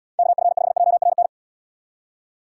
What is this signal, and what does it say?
Morse code, heard as a keyed tone, about 72 wpm, 700 Hz. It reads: KF5JRN